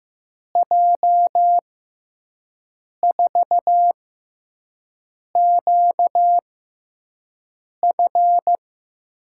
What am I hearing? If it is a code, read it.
Morse code, heard as a keyed tone, about 15 wpm, spaced as for 5 wpm, 700 Hz. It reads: J4QF
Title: USA Call Signs - 6 Characters - 6x Character Spacing - 15wpm